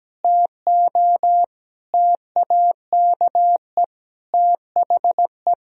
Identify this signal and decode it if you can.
Morse code, heard as a keyed tone, about 17 wpm, 700 Hz. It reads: TO TAKE THE
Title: Sentences from Top 1000 Words - Review Part 3 - 17wpm